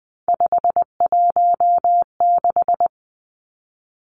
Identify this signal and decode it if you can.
Morse code, heard as a keyed tone, about 20 wpm, 700 Hz. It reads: 516
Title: Three Challenging Letter-Number Combinations 20wpm